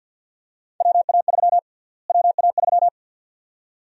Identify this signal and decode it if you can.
Morse code, heard as a keyed tone, about 52 wpm, 700 Hz. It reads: WA3 WA3
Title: USA Call Sign Prefixes - Repeated 2x - 50wpm